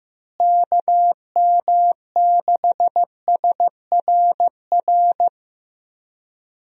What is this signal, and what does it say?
Morse code, heard as a keyed tone, about 15 wpm, 700 Hz. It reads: KM6SRR
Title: USA Call Signs - 6 Characters - Repeated 3x - 15wpm